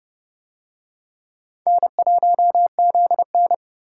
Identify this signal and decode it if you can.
Morse code, heard as a keyed tone, about 29 wpm, 700 Hz. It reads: N1ZN